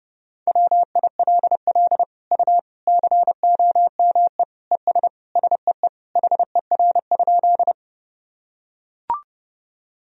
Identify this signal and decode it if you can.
Morse code, heard as a keyed tone, about 32 wpm, 700 Hz. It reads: WILL U COME ES SEE HER?
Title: Sentences from Top 100 Words and Abbreviations 30wpm